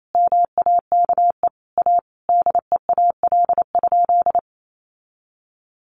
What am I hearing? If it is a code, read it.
Morse code, heard as a keyed tone, about 28 wpm, 700 Hz. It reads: MAKE A DEAL?
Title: Sentences from Top 1400 Words 28wpm